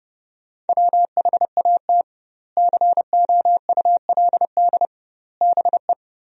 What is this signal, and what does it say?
Morse code, heard as a keyed tone, about 30 wpm, 700 Hz. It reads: WHAT COULD BE
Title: 2x Word-Spacing - Sentences from Top 400 Words - 30wpm